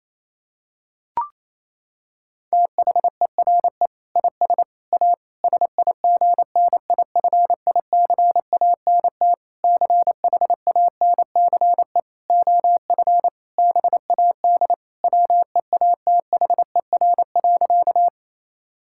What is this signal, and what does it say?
Morse code, heard as a keyed tone, about 28 wpm, 700 Hz. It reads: THERE IS A SIGNIFICANT CHANCE OF BAD WEATHER.
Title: Sentences from Top 1000 Words - Review Part 6 - 28wpm